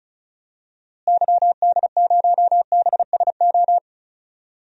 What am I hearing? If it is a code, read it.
Morse code, heard as a keyed tone, about 36 wpm, 700 Hz. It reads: YD0BSO
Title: DXCC-100 Call Signs - 6 Characters - Repeated 3x - 35wpm